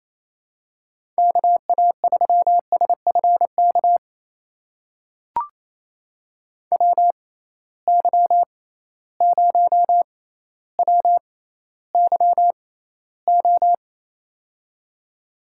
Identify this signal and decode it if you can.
Morse code, heard as a keyed tone, about 28 wpm, spaced as for 14 wpm, 700 Hz. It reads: KA3SFK WY0WYO
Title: USA Call Signs - 6 Characters - 6x Character Spacing - 28wpm